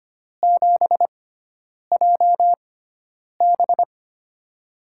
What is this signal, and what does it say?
Morse code, heard as a keyed tone, about 25 wpm, spaced as for 9 wpm, 700 Hz. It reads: 7JB
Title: USA Call Signs - 4 Characters - 6x Character Spacing - 25wpm